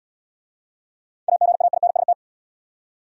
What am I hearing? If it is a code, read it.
Morse code, heard as a keyed tone, about 65 wpm, 690 Hz. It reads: AGAINST